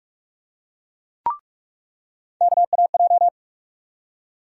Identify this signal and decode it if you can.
Morse code, heard as a keyed tone, about 46 wpm, 700 Hz. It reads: KAJ
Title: USA Call Sign Postfixes - 3 Letters 45wpm